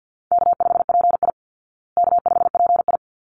Morse code, heard as a keyed tone, about 49 wpm, 700 Hz. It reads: K5PI K5PI
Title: Easy Call Signs Repeated 2x - 50wpm